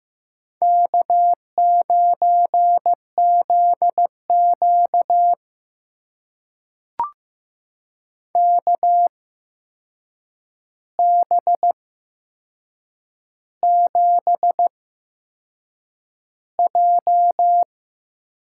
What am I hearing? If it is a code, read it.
Morse code, heard as a keyed tone, about 15 wpm, spaced as for 6 wpm, 700 Hz. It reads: K9ZQ KB7J
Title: USA Call Signs - 8x Character Spacing - 15wpm